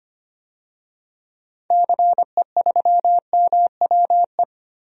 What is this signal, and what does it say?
Morse code, heard as a keyed tone, about 25 wpm, 700 Hz. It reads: CE3MWE
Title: DXCC-100 Call Signs - 6 Characters - 25wpm